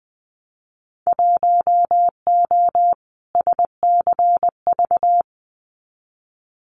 Morse code, heard as a keyed tone, about 20 wpm, 700 Hz. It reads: 1O SCV
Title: Winter Field Day Exchanges 20wpm